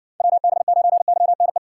Morse code, heard as a keyed tone, about 60 wpm, 700 Hz. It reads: WB9YNE